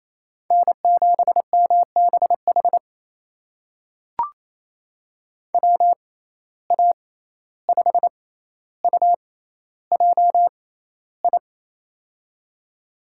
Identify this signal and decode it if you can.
Morse code, heard as a keyed tone, about 28 wpm, spaced as for 14 wpm, 700 Hz. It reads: N7MBH WA5UJI
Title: USA Call Signs - 6x Character Spacing - 28wpm